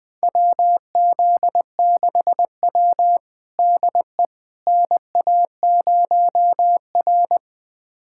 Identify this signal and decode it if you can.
Morse code, heard as a keyed tone, about 20 wpm, 700 Hz. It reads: WZ6W DE NA0R
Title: USA Callsign DE Callsign - 4 Characters - 20wpm